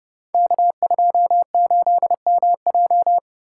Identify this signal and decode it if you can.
Morse code, heard as a keyed tone, about 30 wpm, 700 Hz. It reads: K28MJ
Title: Code Groups - 5 Letter:Numbers - 30wpm